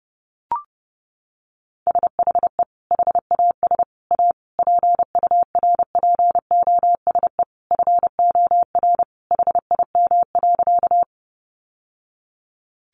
Morse code, heard as a keyed tone, about 30 wpm, 700 Hz. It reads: SHE HAS A PURPOSE FOR HIM.
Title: Sentences from Top 800 Words 30wpm